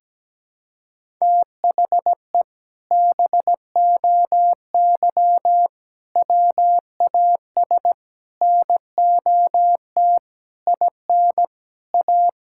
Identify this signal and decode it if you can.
Morse code, heard as a keyed tone, about 17 wpm, 700 Hz. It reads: THE BOY WAS NOT IN A
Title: Sentences from Top 1000 Words - Review Part 6 - 17wpm